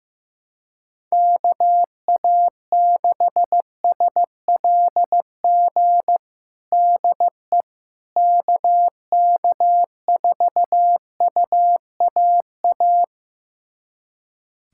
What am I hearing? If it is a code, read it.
Morse code, heard as a keyed tone, about 15 wpm, 700 Hz. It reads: KA6SLG DE KK4UAA
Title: USA Callsign DE Callsign - 6 Characters - 15wpm